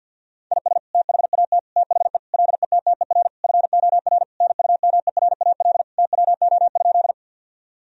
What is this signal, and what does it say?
Morse code, heard as a keyed tone, about 50 wpm, 700 Hz. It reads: IS THAT THE LETTER FOR NUMERAL TWO?
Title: Sentences from Top 1000 Words - Review Part 2 - 50wpm